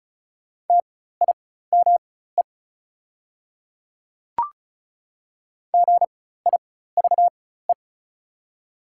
Morse code, heard as a keyed tone, about 36 wpm, spaced as for 16 wpm, 700 Hz. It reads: TIME GIVE
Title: 4x Character Spacing - Top 200 Words - 35wpm